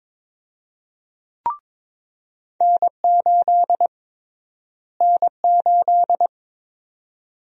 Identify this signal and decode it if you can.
Morse code, heard as a keyed tone, about 22 wpm, 700 Hz. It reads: N8 N8
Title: USA Call Sign Prefixes - Repeated 2x - 22wpm